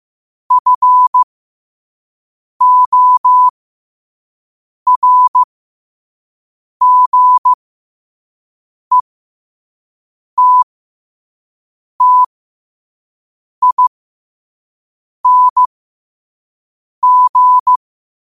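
Morse code, heard as a keyed tone, about 15 wpm, spaced as for 5 wpm, 1000 Hz. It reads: FORGETTING
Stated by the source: Morse code sent at 5 wpm for learning.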